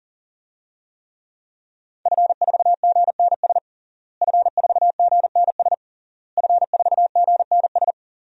Morse code, heard as a keyed tone, about 40 wpm, 700 Hz. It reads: F4GNS F4GNS F4GNS
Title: DXCC-100 Call Signs - 5 Characters - Repeated 3x - 40wpm